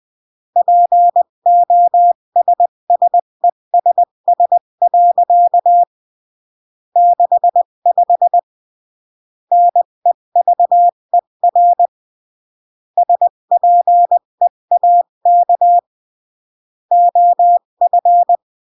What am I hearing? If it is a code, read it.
Morse code, heard as a keyed tone, about 20 wpm, 700 Hz. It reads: POSSESS. 65 NEVER SPEAK OF
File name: CW / jodorowsky / 82 commandments 20-15